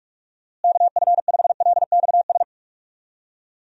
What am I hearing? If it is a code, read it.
Morse code, heard as a keyed tone, about 46 wpm, 700 Hz. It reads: KF5LXS
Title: USA Call Signs - 6 Characters - Repeated 2x - 45wpm